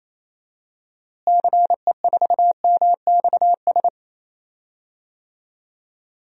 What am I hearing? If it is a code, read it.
Morse code, heard as a keyed tone, about 28 wpm, 700 Hz. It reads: CE4MXS
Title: DXCC-100 Call Signs - 6 Characters - Repeated 3x - 28wpm